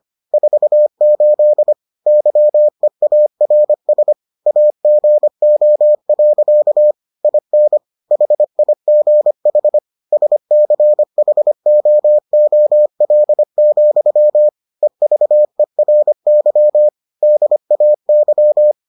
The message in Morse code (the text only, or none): 48 YEARS AGO. IN HIGH SCHOOL, EVERY DAY